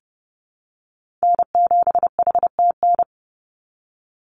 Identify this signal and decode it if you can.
Morse code, heard as a keyed tone, about 32 wpm, 700 Hz. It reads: N7HTN
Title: USA Call Signs 30wpm